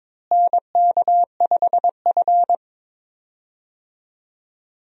Morse code, heard as a keyed tone, about 22 wpm, 700 Hz. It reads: NK5F
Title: USA Call Signs - 4 Characters - Repeated 3x - 22wpm